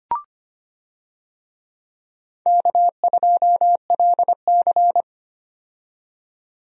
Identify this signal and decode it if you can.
Morse code, heard as a keyed tone, about 25 wpm, 700 Hz. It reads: K2LC
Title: USA Call Signs - 4 Characters - 25wpm